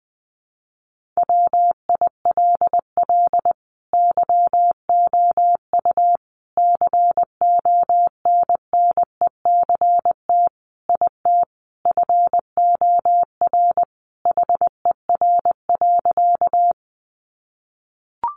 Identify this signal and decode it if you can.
Morse code, heard as a keyed tone, about 20 wpm, 700 Hz. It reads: WILL YOU CONNECT IT FOR HER.
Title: Sentences from Top 1500 Words 20wpm